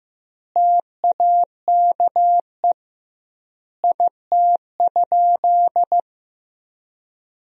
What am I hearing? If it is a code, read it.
Morse code, heard as a keyed tone, about 15 wpm, 700 Hz. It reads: TAKE IT?
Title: 2x Word-Spacing - Sentences from Top 200 Words - 15wpm